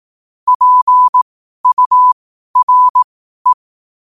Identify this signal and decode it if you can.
Morse code, heard as a keyed tone, about 18 wpm, spaced as for 12 wpm, 1000 Hz. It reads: PURE
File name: qotdfiles / 12wpm